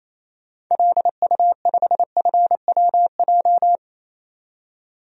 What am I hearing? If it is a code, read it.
Morse code, heard as a keyed tone, about 28 wpm, 700 Hz. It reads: LU5FWJ
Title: DXCC-100 Call Signs - 6 Characters - 28wpm